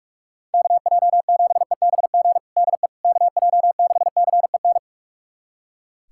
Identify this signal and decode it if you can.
Morse code, heard as a keyed tone, about 44 wpm, 700 Hz. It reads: KJ7EBG DE KJ6CEN
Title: USA Callsign DE Callsign - 6 Characters - 45wpm